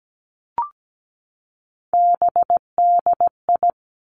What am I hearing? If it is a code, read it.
Morse code, heard as a keyed tone, about 17 wpm, 700 Hz. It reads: BDI
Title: Three Challenging Letter-Number Combinations 17wpm